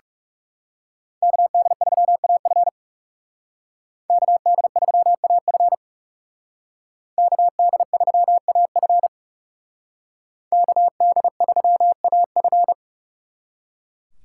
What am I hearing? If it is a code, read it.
Morse code, heard as a keyed tone, about 38 wpm, 700 Hz. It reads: KD3AF KD3AF KD3AF KD3AF